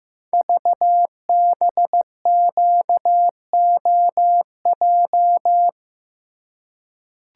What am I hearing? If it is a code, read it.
Morse code, heard as a keyed tone, about 15 wpm, 700 Hz. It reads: VBQOJ
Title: Code Groups - 5 Letters - 15wpm